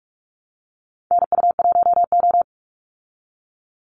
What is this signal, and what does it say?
Morse code, heard as a keyed tone, about 44 wpm, 700 Hz. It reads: NU1O
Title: Call Signs Repeated 2x - 45wpm